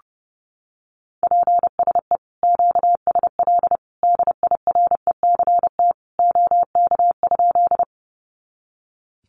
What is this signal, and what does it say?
Morse code, heard as a keyed tone, about 30 wpm, 700 Hz. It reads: PSE QSL DIRECT OK?
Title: Common QSO Phrases 30wpm